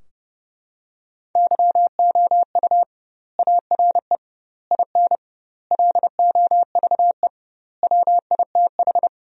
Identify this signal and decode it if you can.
Morse code, heard as a keyed tone, about 30 wpm, 700 Hz. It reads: YOU ARE IN LOVE WITH
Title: 2x Word-Spacing - Sentences from Top 400 Words - 30wpm